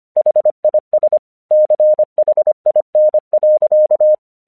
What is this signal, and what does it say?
Morse code, heard as a keyed tone, about 25 wpm, 610 Hz. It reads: HIS CHIN.
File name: 1 English 25wpm / AQC32